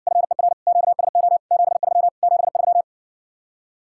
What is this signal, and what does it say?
Morse code, heard as a keyed tone, about 70 wpm, 700 Hz. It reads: FER QSO 73 73